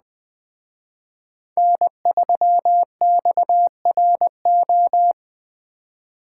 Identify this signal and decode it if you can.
Morse code, heard as a keyed tone, about 20 wpm, 700 Hz. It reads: N3XRO
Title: Speed Racing - USA 1x3 Call Signs 17wpm